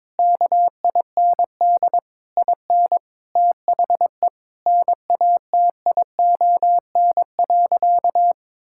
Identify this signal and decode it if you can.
Morse code, heard as a keyed tone, about 22 wpm, 700 Hz. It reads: KIND IN THE NATION.
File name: Sentences with 7-Word Phrases - 22wpm